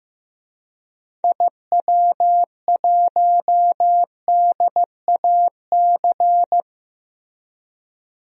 Morse code, heard as a keyed tone, about 15 wpm, 700 Hz. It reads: IW1DAC
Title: DXCC-100 Call Signs - 6 Characters - Repeated 2x - 15wpm